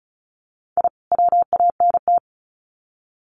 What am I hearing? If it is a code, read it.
Morse code, heard as a keyed tone, about 34 wpm, 700 Hz. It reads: I WANT
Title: Sets of 2 Words 35wpm